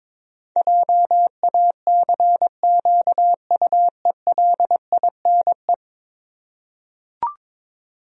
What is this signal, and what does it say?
Morse code, heard as a keyed tone, about 22 wpm, 700 Hz. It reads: JACQUELINE
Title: USA Common Names - 22wpm